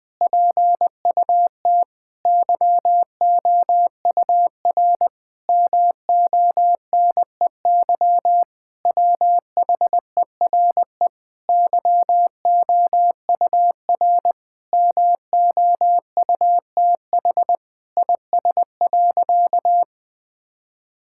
Morse code, heard as a keyed tone, about 20 wpm, 700 Hz. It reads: PUT YOUR MONEY WHERE YOUR MOUTH IS.
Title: Sentences with 7-Word Phrases - Spoken Beginning - 20wpm